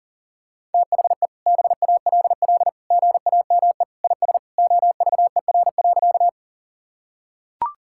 Morse code, heard as a keyed tone, about 40 wpm, 700 Hz. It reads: THE BALL GAME IS OVER.
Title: Sentences from Top 1000 Words - Review Part 6 - 40wpm